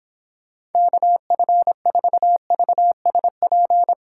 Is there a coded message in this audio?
Yes: Morse code from 0.8 to 3.9 s, heard as a keyed tone, about 31 wpm, 700 Hz. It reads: KF4VSP